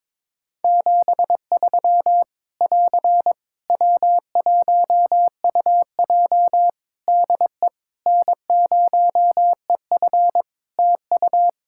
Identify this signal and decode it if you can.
Morse code, heard as a keyed tone, about 22 wpm, 700 Hz. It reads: 73 <AR> W1UJ DE N0EF TU
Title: Practice QSO - Exchange 4 - Ending - 22wpm